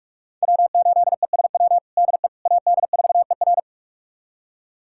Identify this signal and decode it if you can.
Morse code, heard as a keyed tone, about 41 wpm, 700 Hz. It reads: W8ESW DE AD4ER
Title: USA Callsign DE Callsign - 5 Characters - 45wpm